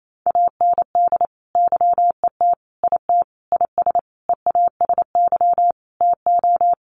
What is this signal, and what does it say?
Morse code, heard as a keyed tone, about 28 wpm, 700 Hz. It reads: AND YET IT IS EASY TO
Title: Sentences from Top 1400 Words 28wpm